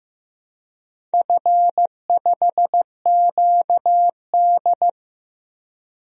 Morse code, heard as a keyed tone, about 15 wpm, 700 Hz. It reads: F5QD
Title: DXCC-100 Call Signs - 4 Characters - 15wpm